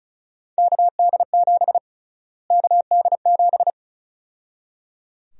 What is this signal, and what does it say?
Morse code, heard as a keyed tone, about 36 wpm, 700 Hz. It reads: KD7 KD7
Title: USA Call Sign Prefixes - Repeated 2x - 35wpm